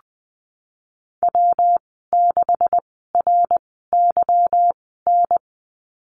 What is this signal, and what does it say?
Morse code, heard as a keyed tone, about 20 wpm, spaced as for 15 wpm, 700 Hz. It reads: W6RYN
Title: 161 Lesson 26g - Letter g - Call Signs 20wpm